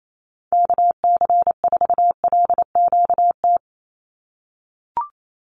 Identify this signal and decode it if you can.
Morse code, heard as a keyed tone, about 28 wpm, 700 Hz. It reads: KC4LQT
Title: USA Call Signs 28wpm